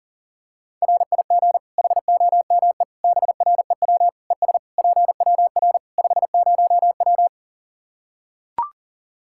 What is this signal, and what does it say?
Morse code, heard as a keyed tone, about 40 wpm, 700 Hz. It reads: RIG HOME BREW ES PWR 50W
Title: Practice QSO - Exchange 2 - Info - 40wpm